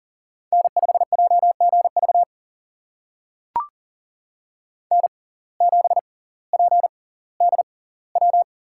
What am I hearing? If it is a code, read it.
Morse code, heard as a keyed tone, about 40 wpm, spaced as for 20 wpm, 700 Hz. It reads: N5JGV N7PDW
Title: USA Call Signs - 5 Characters - 6x Character Spacing - 40wpm